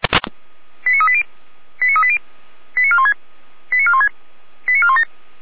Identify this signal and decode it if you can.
Five-tone selective call (ZVEI), heard as quick runs of five tones, tones of about 70 ms. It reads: 89380 89380 89416 89416 89417
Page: SDR/ZVEI